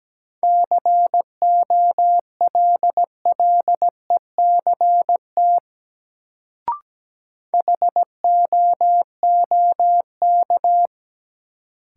Morse code, heard as a keyed tone, about 17 wpm, 700 Hz. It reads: COLLECT HOOK
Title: Sentences from Top 1600 Words - New Word Drill - 17wpm